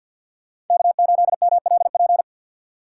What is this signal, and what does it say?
Morse code, heard as a keyed tone, about 48 wpm, 700 Hz. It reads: K8MLP
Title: USA Call Signs - 5 Characters - 6x Character Spacing - 50wpm